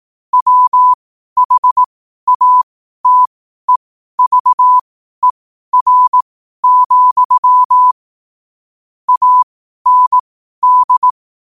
Morse code, heard as a keyed tone, about 18 wpm, spaced as for 12 wpm, 1000 Hz. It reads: WHATEVER, AND
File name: qotdfiles / 12wpm